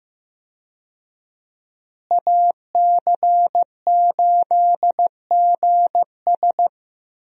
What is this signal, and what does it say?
Morse code, heard as a keyed tone, about 15 wpm, 700 Hz. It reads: AC8GS
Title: USA Call Signs - 5 Characters - Repeated 2x - 15wpm